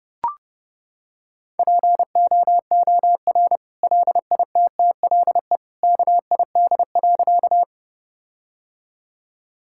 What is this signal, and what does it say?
Morse code, heard as a keyed tone, about 30 wpm, 700 Hz. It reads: POOR LITTLE KID.
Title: Sentences from Top 1600 Words 30wpm